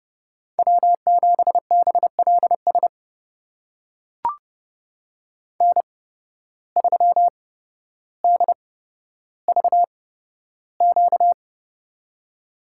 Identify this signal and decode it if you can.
Morse code, heard as a keyed tone, about 30 wpm, spaced as for 13 wpm, 700 Hz. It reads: W7BLS N3DVQ
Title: USA Call Signs - 5 Characters - 8x Character Spacing - 30wpm